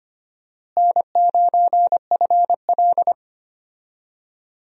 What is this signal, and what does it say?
Morse code, heard as a keyed tone, about 25 wpm, 700 Hz. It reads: N9FL
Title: USA Call Signs - 4 Characters - Repeated 3x - 25wpm